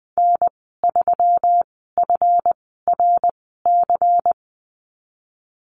Morse code, heard as a keyed tone, about 20 wpm, spaced as for 15 wpm, 700 Hz. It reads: N3FRC
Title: 173 Lesson 28g - Number 9 - Call Signs 20wpm